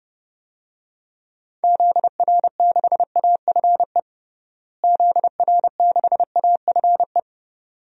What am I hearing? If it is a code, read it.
Morse code, heard as a keyed tone, about 32 wpm, 700 Hz. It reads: ZR6AFE ZR6AFE
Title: DXCC-100 Call Signs - 6 Characters - Repeated 2x - 30wpm